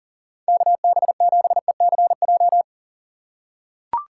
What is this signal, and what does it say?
Morse code, heard as a keyed tone, about 40 wpm, 700 Hz. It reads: KB7ECJ
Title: USA Call Signs - 6 Characters - 3x Character Spacing - 40wpm